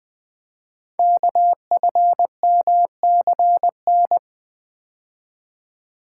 Morse code, heard as a keyed tone, about 20 wpm, 700 Hz. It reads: KFMCN
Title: Code Groups - 5 Letters - 20wpm